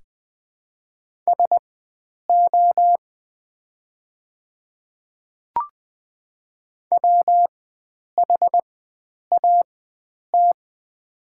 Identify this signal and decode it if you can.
Morse code, heard as a keyed tone, about 20 wpm, spaced as for 9 wpm, 700 Hz. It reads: SO WHAT
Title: 4x Character Spacing - Top 300 Words - 20wpm